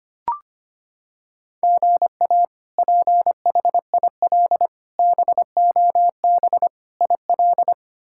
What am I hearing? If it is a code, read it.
Morse code, heard as a keyed tone, about 25 wpm, 700 Hz. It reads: GA PHIL BOB IL
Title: K1USN SST QSO Practice - 25wpm